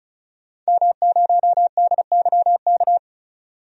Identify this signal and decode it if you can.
Morse code, heard as a keyed tone, about 36 wpm, 700 Hz. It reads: M0DYK
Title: DXCC-100 Call Signs - 5 Characters - 35wpm